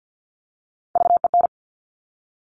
Morse code, heard as a keyed tone, about 50 wpm, 700 Hz. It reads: VEN
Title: Top Three Letter Combinations in Words 50wpm